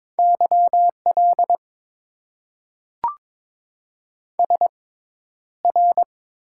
Morse code, heard as a keyed tone, about 22 wpm, 700 Hz. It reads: Y L S R
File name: Top 90 QSO Elements - 6x Character Spacing - 22wpm